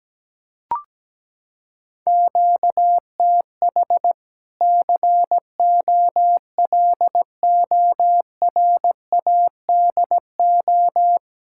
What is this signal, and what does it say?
Morse code, heard as a keyed tone, about 17 wpm, 700 Hz. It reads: QTH COLORADO
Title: Practice QSO - Exchange 1 - Beginning - 17wpm